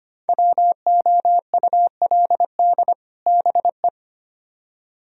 Morse code, heard as a keyed tone, about 30 wpm, 700 Hz. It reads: WOULD BE
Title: Speed-Racing - Sets of 2 Words 25wpm